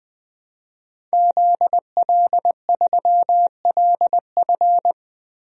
Morse code, heard as a keyed tone, about 20 wpm, 700 Hz. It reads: ZL3LF
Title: DXCC-100 Call Signs - 5 Characters - 20wpm